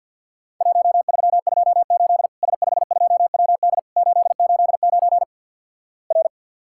Morse code, heard as a keyed tone, about 50 wpm, 700 Hz. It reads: 1328 S52WD 889 R
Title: Worked All Europe DX Contest - QTC Exchange - 50wpm